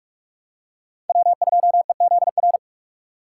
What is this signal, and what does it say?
Morse code, heard as a keyed tone, about 47 wpm, 700 Hz. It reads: W2EZR
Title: USA Call Signs - 5 Characters - 8x Character Spacing - 45wpm